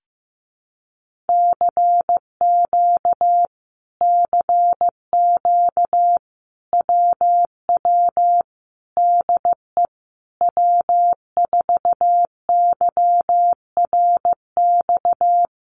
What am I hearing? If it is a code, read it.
Morse code, heard as a keyed tone, about 15 wpm, 700 Hz. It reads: CQ CQ WW DE W4YRX
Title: Common QSO Phrases Encore 15wpm